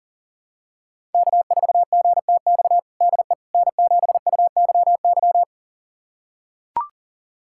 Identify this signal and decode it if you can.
Morse code, heard as a keyed tone, about 40 wpm, 700 Hz. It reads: K4GTX DE N7UYY
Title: USA Callsign DE Callsign 40wpm